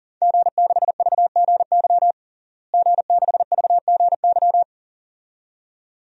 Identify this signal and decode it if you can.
Morse code, heard as a keyed tone, about 40 wpm, 700 Hz. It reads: G6VGY G6VGY
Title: DXCC-100 Call Signs - 5 Characters - Repeated 2x - 40wpm